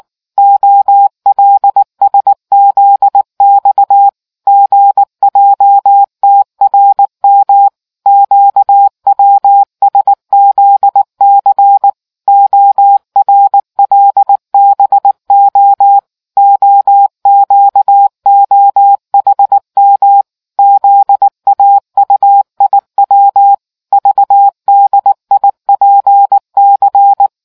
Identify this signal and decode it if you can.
Morse code, heard as a keyed tone, about 19 wpm, 800 Hz. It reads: OLSZX GJTRM QWSZC ORLBO OQOHM ZAUIW VDIPC